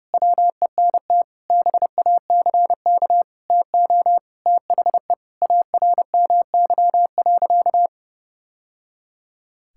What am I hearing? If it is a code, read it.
Morse code, heard as a keyed tone, about 30 wpm, 700 Hz. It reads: WENT BACK TO THE ARMY.
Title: Sentences from Top 1000 Words - Review Part 6 - 30wpm